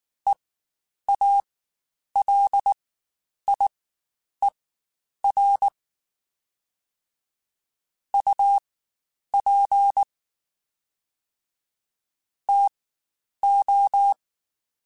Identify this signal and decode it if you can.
Morse code, heard as a keyed tone, about 19 wpm, spaced as for 9 wpm, 800 Hz. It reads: EALIER UP TO
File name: Spaced Characters / Spaced Characters-mp3s / Weather5 5 sp @18